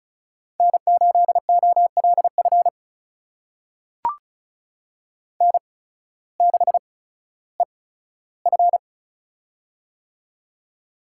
Morse code, heard as a keyed tone, about 36 wpm, spaced as for 16 wpm, 700 Hz. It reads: N8OLF N6EF
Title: USA Call Signs - 8x Character Spacing - 35wpm